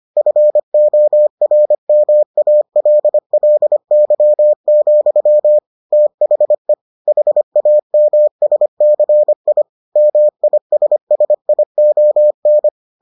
Morse code, heard as a keyed tone, about 25 wpm, 600 Hz. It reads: FORMALLY, THE HAMSCI MISSION